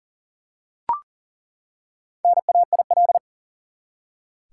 Morse code, heard as a keyed tone, about 40 wpm, 700 Hz. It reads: NAIL
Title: Four Letter Words - 40wpm